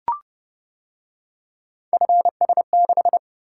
Morse code, heard as a keyed tone, about 29 wpm, 700 Hz. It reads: FS6